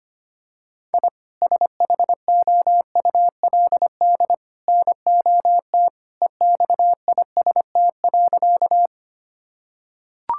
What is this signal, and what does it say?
Morse code, heard as a keyed tone, about 25 wpm, 700 Hz. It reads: I SHOULD NOT EXIST.
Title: Sentences from Top 1000 Words - Review Part 5 - 25wpm